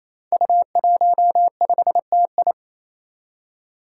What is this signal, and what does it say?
Morse code, heard as a keyed tone, about 28 wpm, 700 Hz. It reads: U15TI
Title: Five Challenging Letter-Number Combinations 28wpm